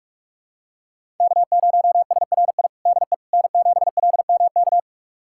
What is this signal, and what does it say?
Morse code, heard as a keyed tone, about 46 wpm, 700 Hz. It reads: K0SRI DE N7LMK
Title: USA Callsign DE Callsign - 5 Characters - 45wpm